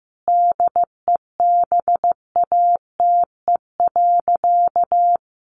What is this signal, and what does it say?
Morse code, heard as a keyed tone, about 15 wpm, 700 Hz. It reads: DEBATE.
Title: Sentences from Top 1500 Words 15wpm